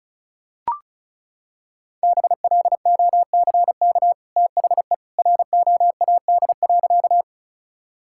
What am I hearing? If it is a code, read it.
Morse code, heard as a keyed tone, about 34 wpm, 700 Hz. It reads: BLOCK THE ROAD.
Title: Sentences from Top 1600 Words 35wpm